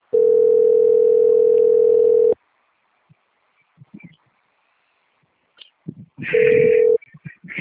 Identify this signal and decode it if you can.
Ringback tone (North America), 440 + 480 Hz.